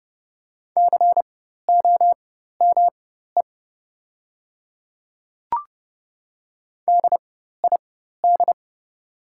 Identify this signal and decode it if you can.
Morse code, heard as a keyed tone, about 30 wpm, spaced as for 14 wpm, 700 Hz. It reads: COME DID
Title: 4x Character Spacing - Top 100 Words - 30wpm